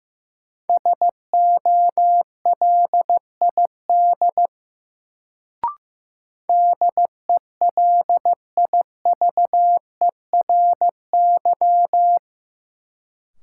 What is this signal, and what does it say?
Morse code, heard as a keyed tone, about 15 wpm, 700 Hz. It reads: SOLID DELIVERY
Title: Sentences from Top 1500 Words - New Word Drill Alt - 15wpm